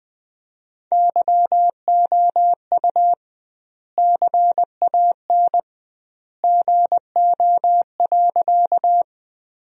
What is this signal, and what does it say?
Morse code, heard as a keyed tone, about 20 wpm, 700 Hz. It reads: YOU CAN GO.
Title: 2x Word-Spacing - Sentences from Top 100 Words - 20wpm